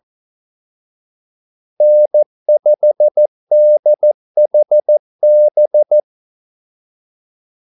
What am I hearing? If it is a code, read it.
Morse code, heard as a keyed tone, about 14 wpm, 600 Hz. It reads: N5DHB
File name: Callsigns-14wpm